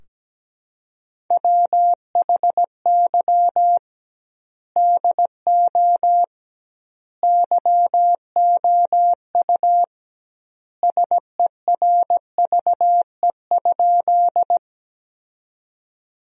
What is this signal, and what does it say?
Morse code, heard as a keyed tone, about 17 wpm, 700 Hz. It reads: WHY DO YOU SERVE?
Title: 2x Word-Spacing - Sentences from Top 400 Words - 17wpm